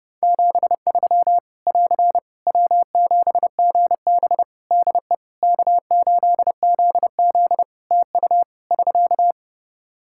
Morse code, heard as a keyed tone, about 31 wpm, 700 Hz. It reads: 73 <AR> W7GB DE K8ZZ TU <SK>
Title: Practice QSO - Exchange 4 - Ending - 30wpm